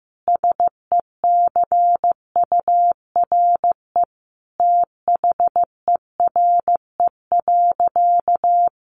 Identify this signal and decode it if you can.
Morse code, heard as a keyed tone, about 15 wpm, 700 Hz. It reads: SECURE THERE.
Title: Sentences from Top 1200 Words 15wpm